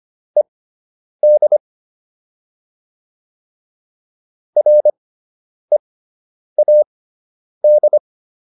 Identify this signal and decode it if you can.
Morse code, heard as a keyed tone, about 25 wpm, spaced as for 9 wpm, 600 Hz. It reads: ED READ